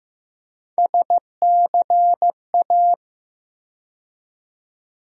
Morse code, heard as a keyed tone, about 15 wpm, 700 Hz. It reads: SCA
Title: USA Call Sign Postfixes - 3 Letters 15wpm